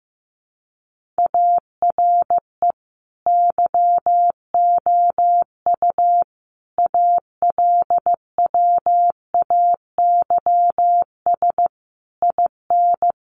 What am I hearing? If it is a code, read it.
Morse code, heard as a keyed tone, about 15 wpm, 700 Hz. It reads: ARE YOU ALWAYS IN
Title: Sentences from Top 1400 Words 15wpm